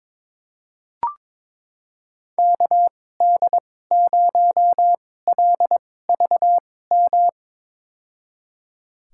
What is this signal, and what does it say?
Morse code, heard as a keyed tone, about 22 wpm, spaced as for 16 wpm, 700 Hz. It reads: KD0LVM
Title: USA Call Signs - 6 Characters - 2x Character Spacing - 22wpm